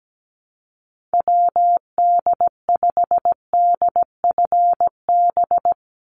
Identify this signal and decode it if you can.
Morse code, heard as a keyed tone, about 17 wpm, 700 Hz. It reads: WD5DFB
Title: USA Call Signs 17wpm